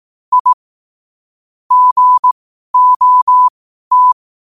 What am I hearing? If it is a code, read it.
Morse code, heard as a keyed tone, about 18 wpm, spaced as for 12 wpm, 1000 Hz. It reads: I GOT
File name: qotdfiles / 12wpm